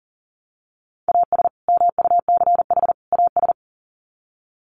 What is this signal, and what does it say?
Morse code, heard as a keyed tone, about 37 wpm, 700 Hz. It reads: AS MUCH AS
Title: Sets of 3 Words 40wpm